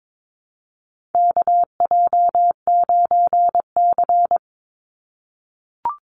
Morse code, heard as a keyed tone, about 22 wpm, 700 Hz. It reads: KJ9C
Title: Call Signs - 22wpm